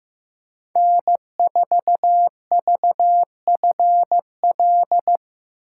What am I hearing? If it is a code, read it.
Morse code, heard as a keyed tone, about 15 wpm, 700 Hz. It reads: N4VFL
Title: USA Call Signs - 5 Characters - 4x Character Spacing - 15wpm